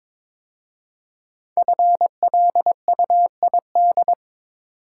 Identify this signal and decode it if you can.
Morse code, heard as a keyed tone, about 22 wpm, 700 Hz. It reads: FLUID
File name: 5 Letter Words - First 3 Letters Given - 22wpm